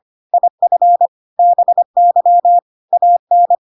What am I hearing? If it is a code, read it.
Morse code, heard as a keyed tone, about 25 wpm, 700 Hz. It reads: IF BY AN
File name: Pinocchio / Pinocchio ch8 25 25 ik1hge